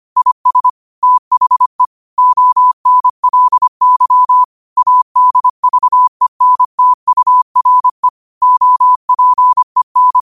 Morse code, heard as a keyed tone, about 25 wpm, 1000 Hz. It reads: IS THE ONLY ADVENTURE OPEN
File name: qotdfiles / 25wpm